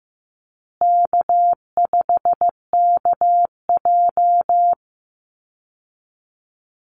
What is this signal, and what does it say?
Morse code, heard as a keyed tone, about 15 wpm, 700 Hz. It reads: K5KJ
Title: Call Signs Repeated 2x - 15wpm